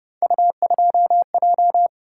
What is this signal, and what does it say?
Morse code, heard as a keyed tone, about 31 wpm, 700 Hz. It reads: U2J
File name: Speed-Racing - Three Letter-Number Combinations - 25wpm